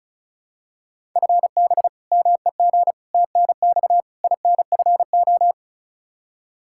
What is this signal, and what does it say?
Morse code, heard as a keyed tone, about 36 wpm, 700 Hz. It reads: FB MEG TNX INFO
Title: Practice QSO - Exchange 3 - More Info - 35wpm